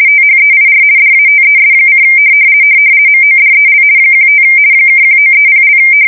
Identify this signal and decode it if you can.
RTTY, about 50 baud. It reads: THEIR DATES: MODEL 15 (1930)
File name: fsk170-50B / teletype-corp